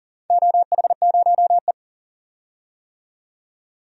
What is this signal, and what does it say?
Morse code, heard as a keyed tone, about 40 wpm, 700 Hz. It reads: OH0E
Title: DXCC-100 Call Signs - 4 Characters - 40wpm